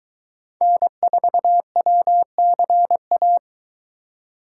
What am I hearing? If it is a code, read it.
Morse code, heard as a keyed tone, about 22 wpm, 700 Hz. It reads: N4WCA